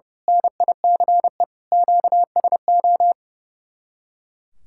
Morse code, heard as a keyed tone, about 28 wpm, 700 Hz. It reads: NICE QSO
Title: Speed-Racing Common QSO Phrases Encore - 30wpm